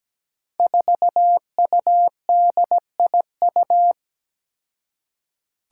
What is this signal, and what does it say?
Morse code, heard as a keyed tone, about 17 wpm, 700 Hz. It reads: 4UDIU
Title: Five Challenging Letter-Number Combinations 17wpm